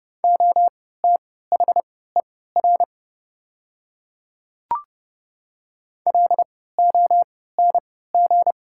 Morse code, heard as a keyed tone, about 30 wpm, spaced as for 17 wpm, 690 Hz. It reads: OTHER LONG
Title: 3x Character Spacing - Top 500 Words - 30wpm